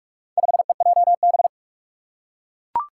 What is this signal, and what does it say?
Morse code, heard as a keyed tone, about 42 wpm, 700 Hz. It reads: 5EJB
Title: Four Challenging Letter-Number Combinations 45wpm